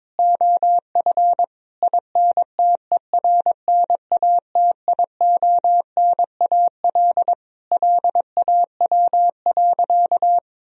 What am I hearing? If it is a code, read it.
Morse code, heard as a keyed tone, about 22 wpm, 700 Hz. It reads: OF INTERNATIONAL LAW.
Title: Sentences from Top 1000 Words - Review Part 4 - 22wpm